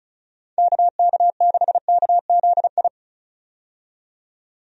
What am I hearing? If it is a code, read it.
Morse code, heard as a keyed tone, about 34 wpm, 700 Hz. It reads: KK6KZI
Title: USA Call Signs - 6 Characters - 35wpm